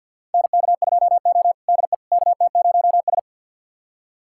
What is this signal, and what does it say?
Morse code, heard as a keyed tone, about 52 wpm, 700 Hz. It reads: NK2O DE KT0S
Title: USA Callsign DE Callsign - 4 Characters - 50wpm